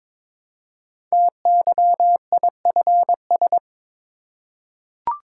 Morse code, heard as a keyed tone, about 21 wpm, 700 Hz. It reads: TYIFS
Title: Code Groups - 5 Letter:Numbers - 22wpm